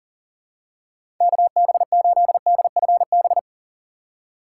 Morse code, heard as a keyed tone, about 40 wpm, 700 Hz. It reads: KB8DFB
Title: USA Call Signs - 6 Characters - 6x Character Spacing - 40wpm